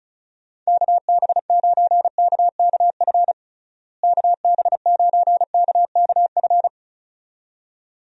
Morse code, heard as a keyed tone, about 36 wpm, 700 Hz. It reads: KB9KKF KB9KKF
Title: USA Call Signs - 6 Characters - Repeated 2x - 35wpm